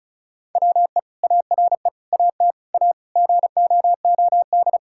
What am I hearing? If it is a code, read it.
Morse code, heard as a keyed tone, about 36 wpm, 700 Hz. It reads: WE ARE AT A GOOD
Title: Sentences from Top 1000 Words - Review Part 6 - 35wpm